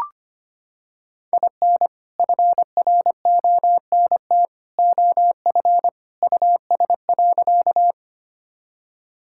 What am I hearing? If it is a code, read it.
Morse code, heard as a keyed tone, about 25 wpm, 700 Hz. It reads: IN FRONT OF US.
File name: practice / 25wpm / Sentences with 2-Word Phrases - 25wpm